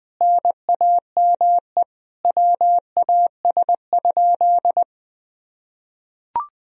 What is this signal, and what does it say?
Morse code, heard as a keyed tone, about 20 wpm, 700 Hz. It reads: NAME WAS?
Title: Sentences from Top 200 Words and Abbreviations 20wpm